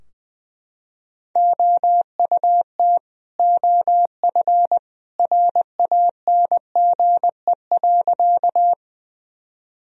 Morse code, heard as a keyed tone, about 21 wpm, 700 Hz. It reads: OUT OF RANGE.
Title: Sentences from Top 1000 Words - Review Part 4 - 20wpm